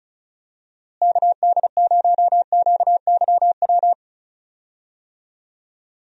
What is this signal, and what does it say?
Morse code, heard as a keyed tone, about 36 wpm, 700 Hz. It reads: KD0QYW
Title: USA Call Signs - 6 Characters - Repeated 2x - 35wpm